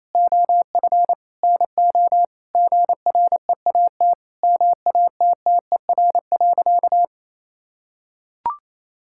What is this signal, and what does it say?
Morse code, heard as a keyed tone, about 28 wpm, 700 Hz. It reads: OF NO GREAT MATTER.
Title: Sentences from Top 1600 Words 28wpm